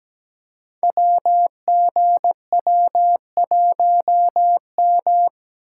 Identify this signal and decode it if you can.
Morse code, heard as a keyed tone, about 17 wpm, 700 Hz. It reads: WGW1M
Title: Code Groups - 5 Letter:Numbers - 17wpm